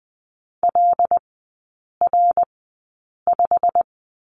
Morse code, heard as a keyed tone, about 20 wpm, spaced as for 8 wpm, 700 Hz. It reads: LR5
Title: 127 Lesson 21e - Number 6 - Three Characters 20wpm